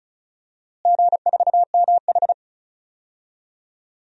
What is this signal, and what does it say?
Morse code, heard as a keyed tone, about 34 wpm, 700 Hz. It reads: G4MH
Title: DXCC-100 Call Signs - 4 Characters - Repeated 3x - 35wpm